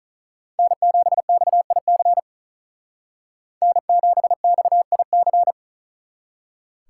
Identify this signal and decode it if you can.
Morse code, heard as a keyed tone, about 42 wpm, 700 Hz. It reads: N7XIC N7XIC
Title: Speed Racing - USA 1x3 Call Signs 35wpm